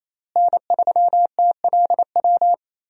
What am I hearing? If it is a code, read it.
Morse code, heard as a keyed tone, about 28 wpm, 700 Hz. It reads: N3TLW
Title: USA Call Signs - 5 Characters - Repeated 3x - 28wpm